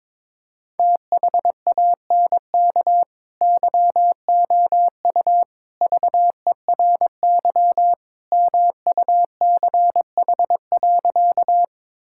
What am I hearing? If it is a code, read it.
Morse code, heard as a keyed tone, about 22 wpm, 700 Hz. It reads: THANK YOU VERY MUCH.
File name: practice / 22wpm / Sentences with 3-Word Phrases - 22wpm